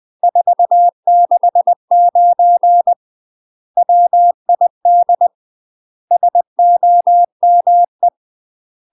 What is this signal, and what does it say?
Morse code, heard as a keyed tone, about 20 wpm, 700 Hz. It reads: 469 WID SOME
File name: UK MASTER QSOs TRAINING / 20WPM / UK MASTER QSOs 0024